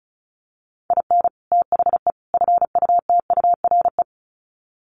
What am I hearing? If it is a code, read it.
Morse code, heard as a keyed tone, about 34 wpm, 700 Hz. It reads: IN THE FUTURE
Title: Sets of 3 Words 35wpm